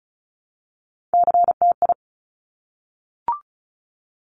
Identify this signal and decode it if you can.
Morse code, heard as a keyed tone, about 36 wpm, 700 Hz. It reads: CTI
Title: Top Three Letter Combinations in Words 35wpm